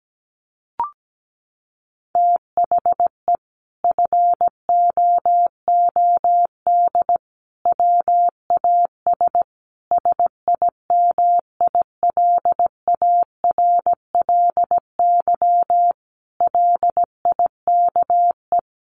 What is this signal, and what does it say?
Morse code, heard as a keyed tone, about 17 wpm, 700 Hz. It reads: THE FOOD WAS SIMILARLY LIKE
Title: Sentences from Top 1400 Words 17wpm